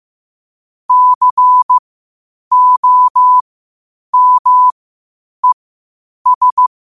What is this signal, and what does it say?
Morse code, heard as a keyed tone, about 15 wpm, spaced as for 8 wpm, 1000 Hz. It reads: COMES
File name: qotdfiles / 8wpm